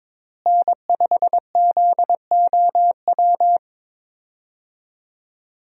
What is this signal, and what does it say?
Morse code, heard as a keyed tone, about 22 wpm, 700 Hz. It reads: N5ZOW
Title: USA Call Signs - 5 Characters - Repeated 2x - 22wpm